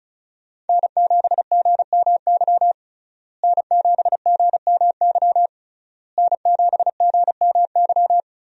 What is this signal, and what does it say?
Morse code, heard as a keyed tone, about 36 wpm, 700 Hz. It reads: N7GMY N7GMY N7GMY
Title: USA Call Signs - 5 Characters - Repeated 3x - 35wpm